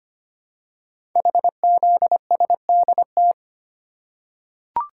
Morse code, heard as a keyed tone, about 25 wpm, 700 Hz. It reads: HZSDT
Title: Code Groups - 5 Letters - 25wpm